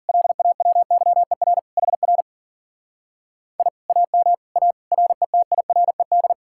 Morse code, heard as a keyed tone, about 50 wpm, 700 Hz. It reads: LAWYER HR I AM A RETIRED
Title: Speed-Racing Common QSO Phrases Encore - 40wpm